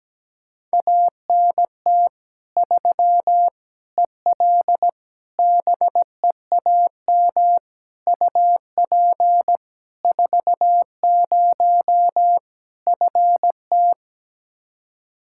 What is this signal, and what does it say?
Morse code, heard as a keyed tone, about 17 wpm, 700 Hz. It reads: ANT 3 EL BEAM UP 40 FT
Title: Practice QSO - Exchange 2 - Info - 17wpm